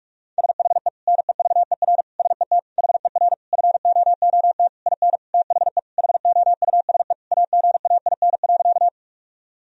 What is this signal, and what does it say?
Morse code, heard as a keyed tone, about 46 wpm, 700 Hz. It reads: SHE NEVER SET HER FOOT IN THE HOUSE AGAIN.
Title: Sentences from Top 1000 Words - Review Part 3 - 45wpm